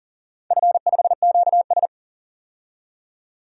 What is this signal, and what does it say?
Morse code, heard as a keyed tone, about 40 wpm, 700 Hz. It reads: F5QS
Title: DXCC-100 Call Signs - 4 Characters - 40wpm